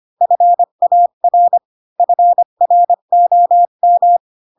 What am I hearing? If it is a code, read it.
Morse code, heard as a keyed tone, about 25 wpm, 700 Hz. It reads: FAR FROM
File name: Pinocchio / Pinocchio ch1 20 25 ik1hge